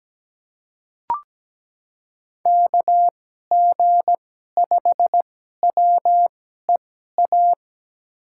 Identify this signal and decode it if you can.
Morse code, heard as a keyed tone, about 17 wpm, spaced as for 12 wpm, 700 Hz. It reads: KG5WEA
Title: USA Call Signs - 6 Characters - 2x Character Spacing - 17wpm